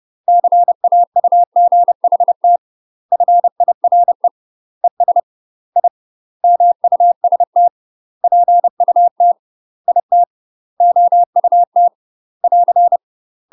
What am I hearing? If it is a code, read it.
Morse code, heard as a keyed tone, about 30 wpm, 700 Hz. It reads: CAUGHT FIRE ES I MUST PUT IT OUT <AR>
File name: UK MASTER QSOs TRAINING/30WPM